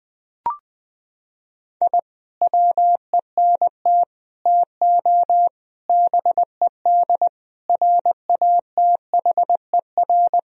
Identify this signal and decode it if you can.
Morse code, heard as a keyed tone, about 20 wpm, 700 Hz. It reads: I WENT TO BED RATHER
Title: Sentences from Top 1000 Words - Review Part 4 - 20wpm